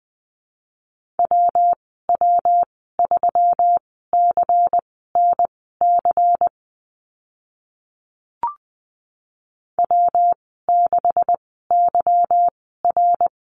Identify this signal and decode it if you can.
Morse code, heard as a keyed tone, about 19 wpm, spaced as for 14 wpm, 700 Hz. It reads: WW3CNC W6YR
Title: 161 Lesson 26g - Letter g - Call Signs 20wpm